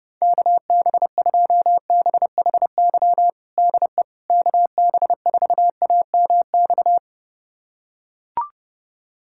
Morse code, heard as a keyed tone, about 30 wpm, 700 Hz. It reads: KB2BHY DE KB4AMX
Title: USA Callsign DE Callsign - 6 Characters - 30wpm